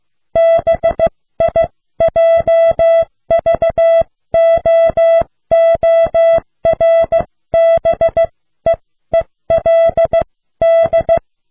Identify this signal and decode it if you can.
Morse code, heard as a keyed tone, about 15 wpm, spaced as for 13 wpm, 660 Hz. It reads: BIJVOORBEELD